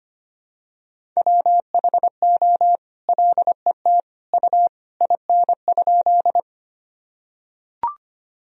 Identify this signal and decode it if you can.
Morse code, heard as a keyed tone, about 25 wpm, 700 Hz. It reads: WHO LET U IN?
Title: Sentences from Top 300 Words and Abbreviations 25wpm